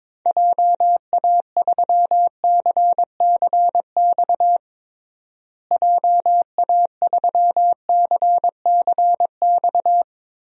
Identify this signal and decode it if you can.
Morse code, heard as a keyed tone, about 22 wpm, 700 Hz. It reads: JA3CCX JA3CCX
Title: DXCC-100 Call Signs - 6 Characters - Repeated 3x - 22wpm